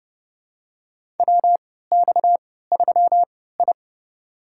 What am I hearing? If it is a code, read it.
Morse code, heard as a keyed tone, about 32 wpm, spaced as for 17 wpm, 700 Hz. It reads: WX3I
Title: USA Call Signs - 3x Character Spacing - 30wpm